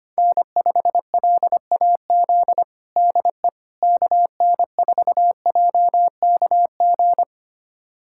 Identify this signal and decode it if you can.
Morse code, heard as a keyed tone, about 25 wpm, 700 Hz. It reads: N5LAZ DE KN4JKG
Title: USA Callsign DE Callsign 25wpm